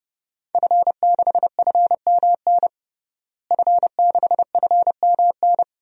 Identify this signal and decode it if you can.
Morse code, heard as a keyed tone, about 30 wpm, 700 Hz. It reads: F6FMN F6FMN
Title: DXCC-100 Call Signs - 5 Characters - Repeated 3x - 30wpm